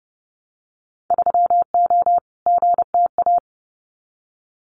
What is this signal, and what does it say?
Morse code, heard as a keyed tone, about 30 wpm, 700 Hz. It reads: 3O GTA
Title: Winter Field Day Exchanges Encore 30wpm